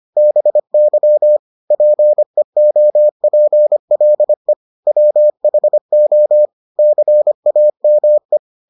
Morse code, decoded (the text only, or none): BY PEOPLE WHO CAME